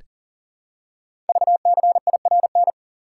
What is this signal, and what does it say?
Morse code, heard as a keyed tone, about 39 wpm, 700 Hz. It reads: VCIRN
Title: Code Groups - 5 Letter:Numbers - 40wpm